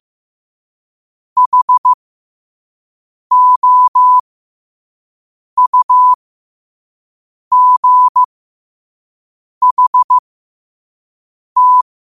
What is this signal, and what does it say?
Morse code, heard as a keyed tone, about 15 wpm, spaced as for 5 wpm, 1000 Hz. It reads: HOUGHT
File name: qotdfiles / 5wpm